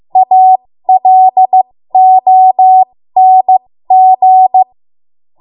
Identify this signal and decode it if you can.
Morse code, heard as a keyed tone, about 15 wpm, spaced as for 13 wpm, 750 Hz. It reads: ALONG
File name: Morse / 13WPM